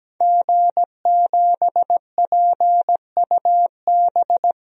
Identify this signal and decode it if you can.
Morse code, heard as a keyed tone, about 17 wpm, 700 Hz. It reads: G7PUB
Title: DXCC-100 Call Signs - 5 Characters - Repeated 3x - 17wpm